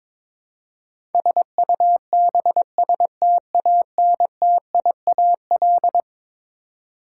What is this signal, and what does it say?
Morse code, heard as a keyed tone, about 22 wpm, 700 Hz. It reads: SUBSTANTIAL
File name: practice / 22wpm / Eleven Letter Words - 22wpm